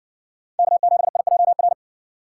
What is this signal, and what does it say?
Morse code, heard as a keyed tone, about 60 wpm, 700 Hz. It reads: K7IJR